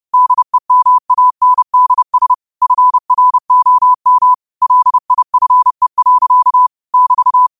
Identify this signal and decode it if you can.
Morse code, heard as a keyed tone, about 30 wpm, 1000 Hz. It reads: DEMANDS FROM LIFE. <BT>